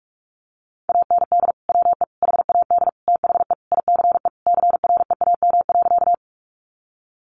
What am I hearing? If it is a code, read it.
Morse code, heard as a keyed tone, about 44 wpm, 700 Hz. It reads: AND WE HAD THE ICE CREAM.
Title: Sentences from Top 1500 Words 45wpm